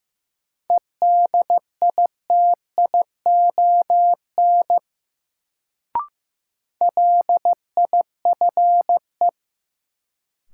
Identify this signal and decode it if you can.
Morse code, heard as a keyed tone, about 16 wpm, 700 Hz. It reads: EDITION LIFE
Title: Sets of 5 Words - New Word Drill - 15wpm